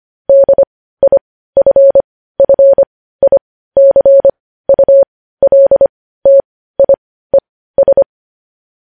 Morse code, heard as a keyed tone, about 25 wpm, spaced as for 15 wpm, 550 Hz. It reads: DIFFICULTIES
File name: podcast / News headlines-1x-25x15